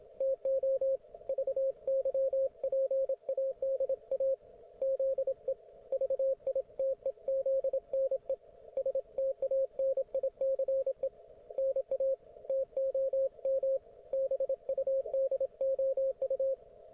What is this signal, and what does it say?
Morse code, heard as a keyed tone, about 27 wpm, 540 Hz, with radio noise. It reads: TO VYPADA ZE VITEZNE STANICE NA TOM BUDOU